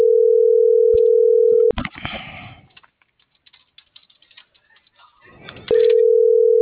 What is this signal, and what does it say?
Ringback tone (North America), 440 + 480 Hz.